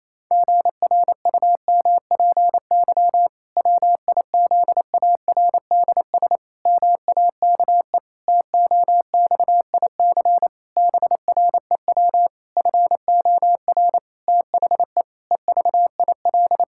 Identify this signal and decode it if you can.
Morse code, heard as a keyed tone, about 28 wpm, 700 Hz. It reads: GRUMPY WIZARDS MAKE TOXIC BREW FOR THE EVIL
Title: Pangram Challenge v2 - 28wpm